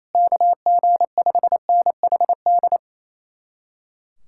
Morse code, heard as a keyed tone, about 36 wpm, 700 Hz. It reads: KG5NHD